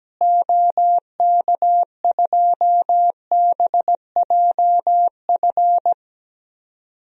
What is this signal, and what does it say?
Morse code, heard as a keyed tone, about 17 wpm, 700 Hz. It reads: OK2BJF
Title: DXCC-100 Call Signs - 6 Characters - Repeated 3x - 17wpm